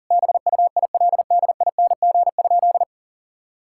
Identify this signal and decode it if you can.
Morse code, heard as a keyed tone, about 40 wpm, 700 Hz. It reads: BUILDING?
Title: Sentences from Top 1000 Words - Review Part 5 - 40wpm